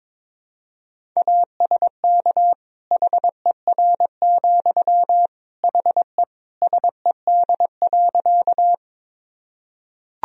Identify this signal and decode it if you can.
Morse code, heard as a keyed tone, about 22 wpm, 700 Hz. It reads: ASK HER, HE SED.
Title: Sentences from Top 200 Words and Abbreviations 22wpm